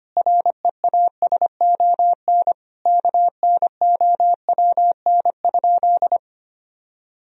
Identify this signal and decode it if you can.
Morse code, heard as a keyed tone, about 25 wpm, 700 Hz. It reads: REASON KNOWN?
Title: Sentences from Top 1000 Words - Review Part 3 - 25wpm